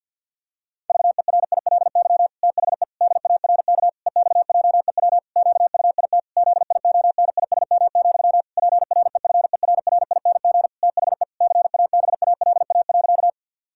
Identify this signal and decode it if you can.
Morse code, heard as a keyed tone, about 48 wpm, 700 Hz. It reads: VERILY THE DARK EXJEW QUIT ZIONISM, PREFERRING THE CABALA.
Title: Pangram Challenge v2 - 50wpm